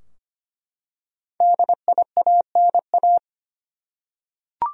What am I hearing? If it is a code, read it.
Morse code, heard as a keyed tone, about 25 wpm, 700 Hz. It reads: DIANA
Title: USA Common Names - Max 5 Letters - 25wpm